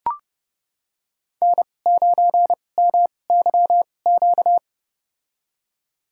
Morse code, heard as a keyed tone, about 30 wpm, spaced as for 22 wpm, 700 Hz. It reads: N9MYQ
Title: USA Call Signs - 5 Characters - 2x Character Spacing - 30wpm